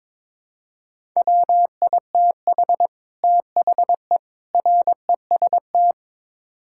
Morse code, heard as a keyed tone, about 22 wpm, 700 Hz. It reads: WITH THE REST
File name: practice / 22wpm / Build A Phrase - 5-Word Phrases - 22wpm